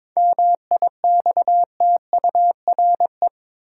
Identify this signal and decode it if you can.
Morse code, heard as a keyed tone, about 21 wpm, 700 Hz. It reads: MIXTURE
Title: Seven Letter Words - 22wpm